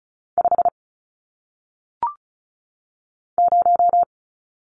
Morse code, heard as a keyed tone, about 36 wpm, 700 Hz. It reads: 5 0
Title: Numbers - 1-Digit 35wpm